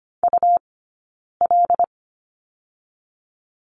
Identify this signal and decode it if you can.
Morse code, heard as a keyed tone, about 25 wpm, spaced as for 9 wpm, 700 Hz. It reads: UL
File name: course / 25:10wpm / 099 Lesson 17d - Letter u - Two Characters 25wpm